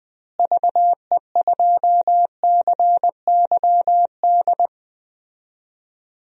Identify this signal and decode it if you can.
Morse code, heard as a keyed tone, about 21 wpm, 700 Hz. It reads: VE2CYD
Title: DXCC-100 Call Signs - 6 Characters - Repeated 2x - 20wpm